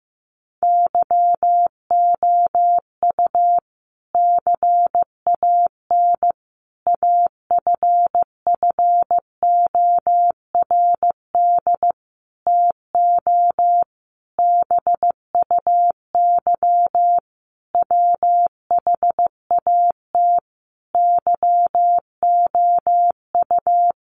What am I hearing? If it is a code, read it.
Morse code, heard as a keyed tone, about 15 wpm, 700 Hz. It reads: YOU CAN AFFORD TO BUY WHAT YOU
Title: Sentences from Top 1100 Words 15wpm